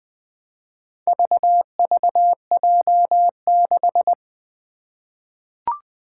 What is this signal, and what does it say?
Morse code, heard as a keyed tone, about 20 wpm, 700 Hz. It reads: VVJ6
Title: Four Challenging Letter-Number Combinations 20wpm